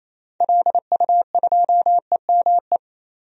Morse code, heard as a keyed tone, about 28 wpm, 700 Hz. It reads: LU2EME
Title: DXCC-100 Call Signs - 6 Characters - 28wpm